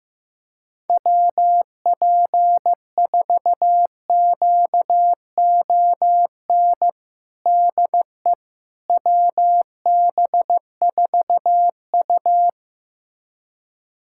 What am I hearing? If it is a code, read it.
Morse code, heard as a keyed tone, about 15 wpm, 700 Hz. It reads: WP4QON DE WB4U
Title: USA Callsign DE Callsign 15wpm